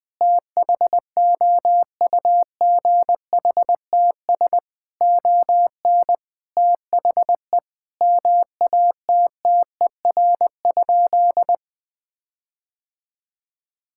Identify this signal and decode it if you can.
Morse code, heard as a keyed tone, about 21 wpm, 700 Hz. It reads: THOUGHTS ON THE MATTER?
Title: Sentences with 6-Word Phrases - 20wpm